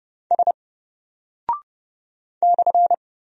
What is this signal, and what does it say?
Morse code, heard as a keyed tone, about 30 wpm, 700 Hz. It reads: S /
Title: Single Character (Letter:Number:Punctuation:Prosign) 30wpm